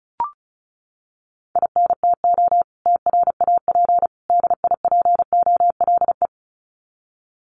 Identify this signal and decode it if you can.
Morse code, heard as a keyed tone, about 34 wpm, 700 Hz. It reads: INTO TRAP DIPOLE
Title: Common QSO Phrases 35wpm